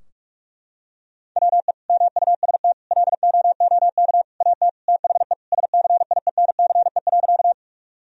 Morse code, heard as a keyed tone, about 46 wpm, 700 Hz. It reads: WE MUST LOOK AT THE SCIENCE.
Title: Sentences from Top 1000 Words - Review Part 2 - 45wpm